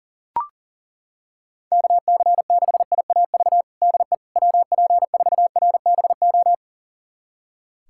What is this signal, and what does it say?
Morse code, heard as a keyed tone, about 40 wpm, 700 Hz. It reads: KC6IAV DE WP4RBO
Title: USA Callsign DE Callsign 40wpm